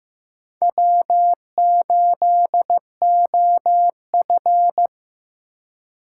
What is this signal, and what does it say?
Morse code, heard as a keyed tone, about 15 wpm, 700 Hz. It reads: W8OF
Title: USA Call Signs - 4 Characters - 2x Character Spacing - 15wpm